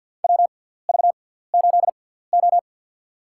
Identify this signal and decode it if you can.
Morse code, heard as a keyed tone, about 49 wpm, spaced as for 17 wpm, 700 Hz. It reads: WV8O
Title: USA Call Signs - 4 Characters - 6x Character Spacing - 50wpm